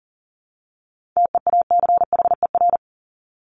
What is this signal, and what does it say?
Morse code, heard as a keyed tone, about 40 wpm, 700 Hz. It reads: TEACHER
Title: Top 751-1000 Words 40wpm